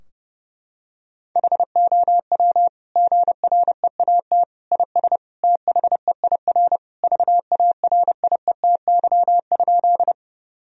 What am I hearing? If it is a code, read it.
Morse code, heard as a keyed tone, about 30 wpm, 700 Hz. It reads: HOW GREAT IS THEIR VARIETY?
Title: Sentences from Top 1000 Words - Review Part 4 - 30wpm